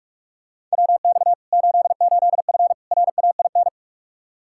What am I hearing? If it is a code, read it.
Morse code, heard as a keyed tone, about 46 wpm, 700 Hz. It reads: WX 88F RAIN
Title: Practice QSO - Exchange 2 - Info - 45wpm